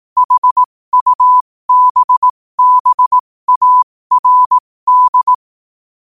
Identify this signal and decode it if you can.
Morse code, heard as a keyed tone, about 18 wpm, spaced as for 15 wpm, 1000 Hz. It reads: HUBBARD
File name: qotdfiles / 15wpm